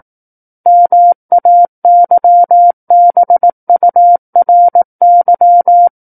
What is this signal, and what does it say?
Morse code, heard as a keyed tone, about 18 wpm, 700 Hz. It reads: MAYBURY
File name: 18 WPM